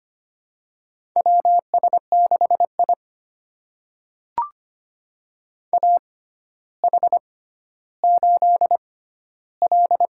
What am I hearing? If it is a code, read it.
Morse code, heard as a keyed tone, about 25 wpm, spaced as for 13 wpm, 700 Hz. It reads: WS6I AH8L
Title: USA Call Signs - 4 Characters - 6x Character Spacing - 25wpm